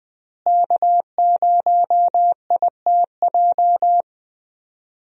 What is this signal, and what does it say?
Morse code, heard as a keyed tone, about 20 wpm, 700 Hz. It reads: K0ITJ
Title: USA Call Signs - 5 Characters - Repeated 3x - 20wpm